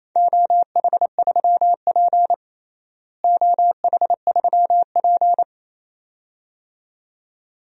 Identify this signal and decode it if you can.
Morse code, heard as a keyed tone, about 28 wpm, 700 Hz. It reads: OH3P OH3P
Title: DXCC-100 Call Signs - 4 Characters - Repeated 2x - 28wpm